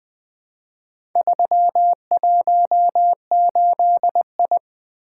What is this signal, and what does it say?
Morse code, heard as a keyed tone, about 20 wpm, 700 Hz. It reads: 318I
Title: Four Letter-Number Combinations - 20wpm